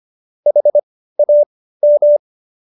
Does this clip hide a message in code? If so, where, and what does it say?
Yes: Morse code from 0.5 to 2.2 s, heard as a keyed tone, about 25 wpm, spaced as for 15 wpm, 590 Hz. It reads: HAM